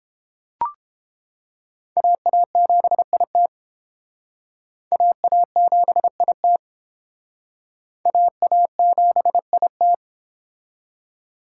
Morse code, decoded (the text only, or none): AA7IT AA7IT AA7IT